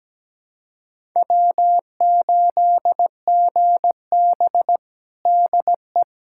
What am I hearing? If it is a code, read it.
Morse code, heard as a keyed tone, about 17 wpm, 700 Hz. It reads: W8GB DE
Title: USA Callsign DE Callsign - 4 Characters - 17wpm